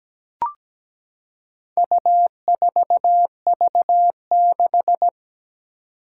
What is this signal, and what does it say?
Morse code, heard as a keyed tone, about 17 wpm, 700 Hz. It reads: U4V6
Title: Four Challenging Letter-Number Combinations 17wpm